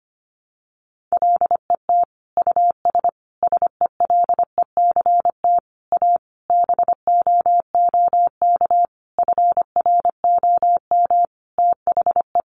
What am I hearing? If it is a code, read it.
Morse code, heard as a keyed tone, about 25 wpm, 700 Hz. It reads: LET US SELECT A BOOK FROM THE
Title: Sentences from Top 1300 Words 25wpm